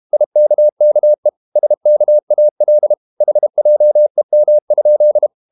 Morse code, heard as a keyed tone, about 32 wpm, 600 Hz. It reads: IKKE SKAL HJEM?
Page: Ferie 32 wpm